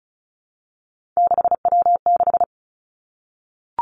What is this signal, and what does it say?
Morse code, heard as a keyed tone, about 36 wpm, 700 Hz. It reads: 6W6
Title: Three Challenging Letter-Number Combinations 35wpm